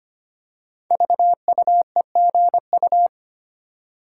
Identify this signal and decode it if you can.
Morse code, heard as a keyed tone, about 25 wpm, 700 Hz. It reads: VUEGU
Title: Code Groups - 5 Letters - 25wpm